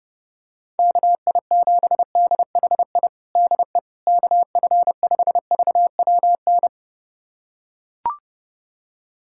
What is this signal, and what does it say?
Morse code, heard as a keyed tone, about 30 wpm, 700 Hz. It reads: KI7DHI DE KF5VWN
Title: USA Callsign DE Callsign - 6 Characters - 30wpm